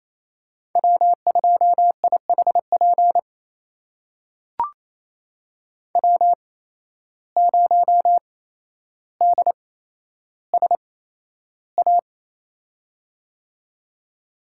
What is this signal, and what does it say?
Morse code, heard as a keyed tone, about 27 wpm, spaced as for 12 wpm, 700 Hz. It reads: W2IHP W0DSA
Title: USA Call Signs - 5 Characters - 8x Character Spacing - 28wpm